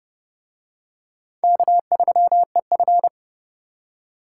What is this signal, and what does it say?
Morse code, heard as a keyed tone, about 30 wpm, 700 Hz. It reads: K3EF